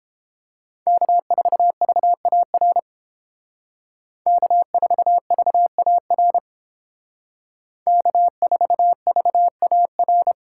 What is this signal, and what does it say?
Morse code, heard as a keyed tone, about 29 wpm, 700 Hz. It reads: K4VAR K4VAR K4VAR